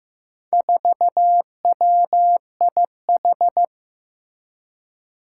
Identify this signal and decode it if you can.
Morse code, heard as a keyed tone, about 15 wpm, 700 Hz. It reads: 4WIH
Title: Four Challenging Letter-Number Combinations 15wpm